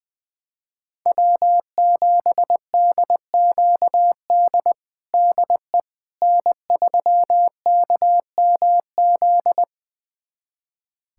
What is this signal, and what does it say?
Morse code, heard as a keyed tone, about 20 wpm, 700 Hz. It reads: W7DQD DE N3KMZ
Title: USA Callsign DE Callsign - 5 Characters - 20wpm